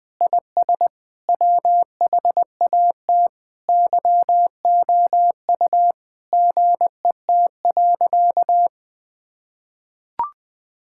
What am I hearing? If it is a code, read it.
Morse code, heard as a keyed tone, about 21 wpm, 700 Hz. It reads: IS WHAT YOU GET.
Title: Sentences with 7-Word Phrases - Spoken Beginning - 20wpm